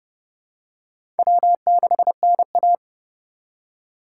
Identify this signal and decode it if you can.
Morse code, heard as a keyed tone, about 32 wpm, 700 Hz. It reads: W6NA
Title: USA Call Signs - 4 Characters - 4x Character Spacing - 30wpm